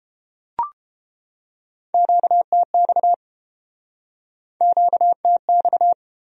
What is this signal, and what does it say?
Morse code, heard as a keyed tone, about 32 wpm, 700 Hz. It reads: QTX QTX